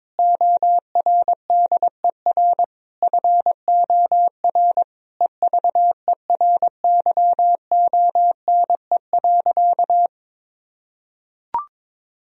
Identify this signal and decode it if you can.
Morse code, heard as a keyed tone, about 22 wpm, 700 Hz. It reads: ORDER FOR EVERYONE.
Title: Sentences from Top 1000 Words - Review Part 3 - 22wpm